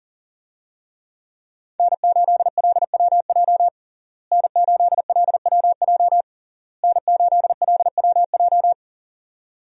Morse code, heard as a keyed tone, about 40 wpm, 700 Hz. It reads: N8LWJ N8LWJ N8LWJ
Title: USA Call Signs - 5 Characters - Repeated 3x - 40wpm